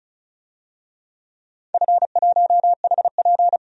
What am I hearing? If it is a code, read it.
Morse code, heard as a keyed tone, about 36 wpm, 700 Hz. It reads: F1HP
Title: DXCC-100 Call Signs - 4 Characters - 35wpm